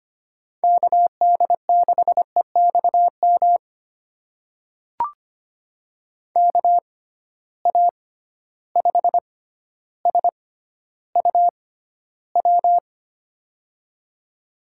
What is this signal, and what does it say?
Morse code, heard as a keyed tone, about 25 wpm, spaced as for 13 wpm, 700 Hz. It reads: KD6EXM KA5SUW
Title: USA Call Signs - 6 Characters - 6x Character Spacing - 25wpm